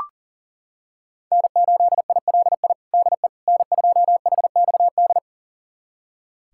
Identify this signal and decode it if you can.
Morse code, heard as a keyed tone, about 40 wpm, 700 Hz. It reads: N8ILI DE N2HXD
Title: USA Callsign DE Callsign - 5 Characters - 40wpm